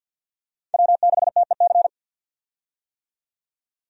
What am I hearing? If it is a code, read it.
Morse code, heard as a keyed tone, about 51 wpm, 700 Hz. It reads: W6TEC
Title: USA Call Signs - 5 Characters - 50wpm